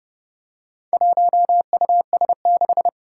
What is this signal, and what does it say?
Morse code, heard as a keyed tone, about 30 wpm, 700 Hz. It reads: 1US6
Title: Four Challenging Letter-Number Combinations 30wpm